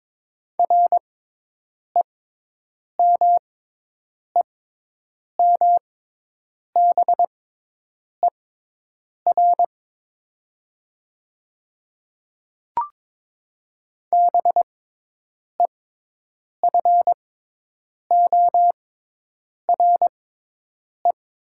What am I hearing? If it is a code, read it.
Morse code, heard as a keyed tone, about 22 wpm, spaced as for 8 wpm, 700 Hz. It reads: REMEMBER BEFORE
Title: 6x Character Spacing - Top 500 Words - 22wpm